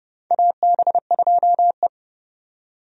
Morse code, heard as a keyed tone, about 30 wpm, 700 Hz. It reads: AB2E
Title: USA Call Signs - 4 Characters - 30wpm